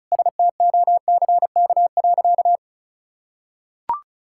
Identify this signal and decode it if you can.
Morse code, heard as a keyed tone, about 36 wpm, 700 Hz. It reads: STOCK.
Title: Sentences from Top 1000 Words - Review Part 5 - 35wpm